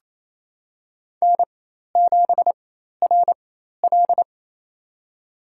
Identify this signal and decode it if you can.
Morse code, heard as a keyed tone, about 28 wpm, spaced as for 13 wpm, 700 Hz. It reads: N7RL
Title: USA Call Signs - 4x Character Spacing - 28wpm